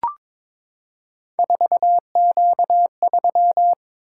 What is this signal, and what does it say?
Morse code, heard as a keyed tone, about 22 wpm, 700 Hz. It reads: 4Q3
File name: practice / 22wpm / Three Letter-Number Combinations - 22wpm